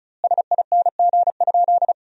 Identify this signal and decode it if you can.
Morse code, heard as a keyed tone, about 34 wpm, 700 Hz. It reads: SING?
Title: 2x Word-Spacing - Sentences from Top 400 Words - 35wpm